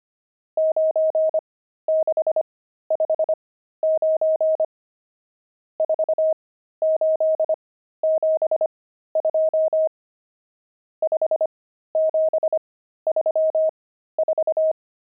Morse code, heard as a keyed tone, about 25 wpm, spaced as for 13 wpm, 630 Hz. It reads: 9659 4872 5734
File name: pangram-fs-13-cs-25-nums10